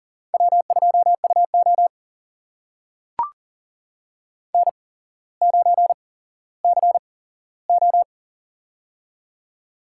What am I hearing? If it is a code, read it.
Morse code, heard as a keyed tone, about 40 wpm, spaced as for 17 wpm, 700 Hz. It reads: W2UO N9CO
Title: USA Call Signs - 4 Characters - 8x Character Spacing - 40wpm